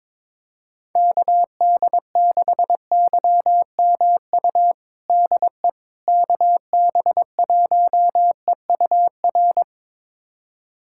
Morse code, heard as a keyed tone, about 22 wpm, 700 Hz. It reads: KD6YMU DE KB1EUR
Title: USA Callsign DE Callsign - 6 Characters - 22wpm